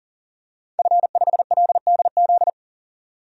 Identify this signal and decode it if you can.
Morse code, heard as a keyed tone, about 37 wpm, 700 Hz. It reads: F5LDZ
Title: DXCC-100 Call Signs - 5 Characters - 40wpm